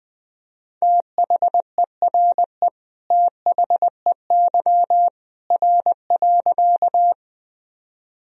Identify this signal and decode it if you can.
Morse code, heard as a keyed tone, about 21 wpm, 700 Hz. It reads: THERE THEY R.
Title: Sentences from Top 100 Words and Abbreviations 20wpm